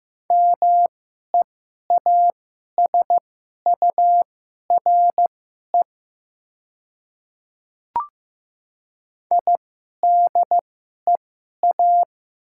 Morse code, heard as a keyed tone, about 15 wpm, spaced as for 11 wpm, 700 Hz. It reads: MEASURE IDEA
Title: 2x Character Spacing - Top 500 Words - 15wpm